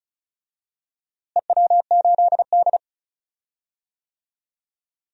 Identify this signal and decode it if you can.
Morse code, heard as a keyed tone, about 34 wpm, 700 Hz. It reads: EW8D
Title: DXCC-100 Call Signs - 4 Characters - Repeated 2x - 35wpm